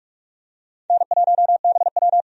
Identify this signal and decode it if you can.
Morse code, heard as a keyed tone, about 46 wpm, 700 Hz. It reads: N1BW
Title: USA Call Signs - 4 Characters - 45wpm